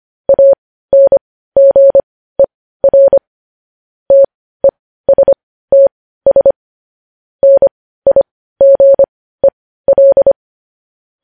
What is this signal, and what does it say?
Morse code, heard as a keyed tone, about 25 wpm, spaced as for 15 wpm, 550 Hz. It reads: ANGER TESTS NIGEL
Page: News Headlines in Morse Code at 15 WPM